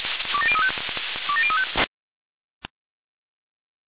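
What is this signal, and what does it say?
Five-tone selective call (ZVEI), heard as quick runs of five tones, tones of about 70 ms. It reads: 27035 27035
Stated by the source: ZVEI five-tone selective call 27084, received on SDR.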